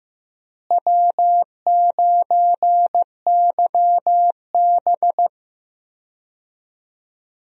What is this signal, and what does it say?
Morse code, heard as a keyed tone, about 16 wpm, 700 Hz. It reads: W9YB
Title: USA Call Signs - 4 Characters - Repeated 2x - 15wpm